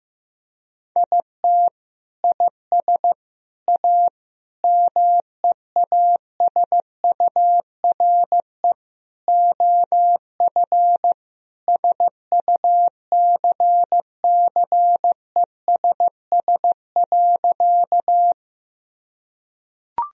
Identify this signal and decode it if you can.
Morse code, heard as a keyed tone, about 15 wpm, 700 Hz. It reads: IT IS A MEASURE OF SUCCESS.
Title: Sentences from Top 1000 Words - Review Part 2 - 15wpm